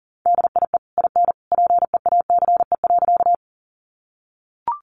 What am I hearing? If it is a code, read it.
Morse code, heard as a keyed tone, about 40 wpm, 700 Hz. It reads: DIE IN PEACE.
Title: Sentences from Top 1500 Words 40wpm